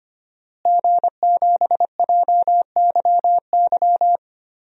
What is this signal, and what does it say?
Morse code, heard as a keyed tone, about 25 wpm, 700 Hz. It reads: G7JYY
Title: DXCC-100 Call Signs - 5 Characters - Repeated 2x - 25wpm